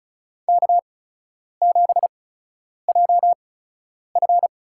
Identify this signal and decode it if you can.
Morse code, heard as a keyed tone, about 34 wpm, spaced as for 10 wpm, 700 Hz. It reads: K7JF
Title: USA Call Signs - 4 Characters - 8x Character Spacing - 35wpm